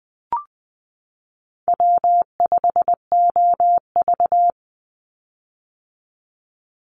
Morse code, heard as a keyed tone, about 20 wpm, 700 Hz. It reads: W5OV
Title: Call Signs - 20wpm